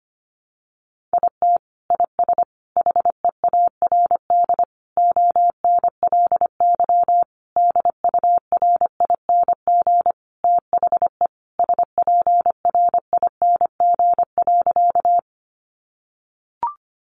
Morse code, heard as a keyed tone, about 25 wpm, 700 Hz. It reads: IT IS HEARD ONLY DURING THE SPRING.
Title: Sentences from Top 1200 Words 25wpm